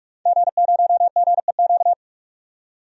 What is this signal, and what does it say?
Morse code, heard as a keyed tone, about 46 wpm, 700 Hz. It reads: G0GEQ
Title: DXCC-100 Call Signs - 5 Characters - 45wpm